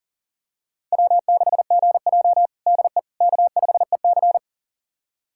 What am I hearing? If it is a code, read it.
Morse code, heard as a keyed tone, about 40 wpm, 700 Hz. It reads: W6GJ DE K5EC
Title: USA Callsign DE Callsign - 4 Characters - 40wpm